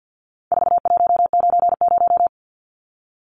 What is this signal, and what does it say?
Morse code, heard as a keyed tone, about 52 wpm, 700 Hz. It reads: 4190
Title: Numbers - 4-Digit 50wpm